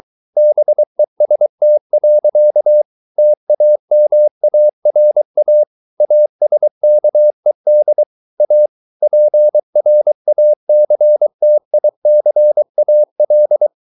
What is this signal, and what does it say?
Morse code, heard as a keyed tone, about 23 wpm, 600 Hz. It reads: BEST. TAMARA ASKED A PRACTICAL